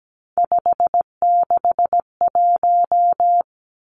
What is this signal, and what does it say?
Morse code, heard as a keyed tone, about 17 wpm, 700 Hz. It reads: 561
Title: Numbers - 3-Digit 17wpm